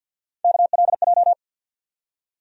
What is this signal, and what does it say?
Morse code, heard as a keyed tone, about 51 wpm, 700 Hz. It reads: KLJ
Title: USA Call Sign Postfixes - 3 Letters 50wpm